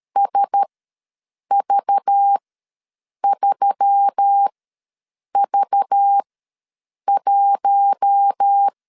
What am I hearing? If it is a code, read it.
Morse code, heard as a keyed tone, about 13 wpm, spaced as for 7 wpm, 800 Hz. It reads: SV3V1
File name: LEZIONI / 5 LBVC / 5 LBV ws 12wpm 3015